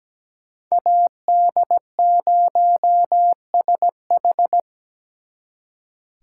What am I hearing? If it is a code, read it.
Morse code, heard as a keyed tone, about 20 wpm, 700 Hz. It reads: AD0SH
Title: Speed Racing - USA Call Signs 17wpm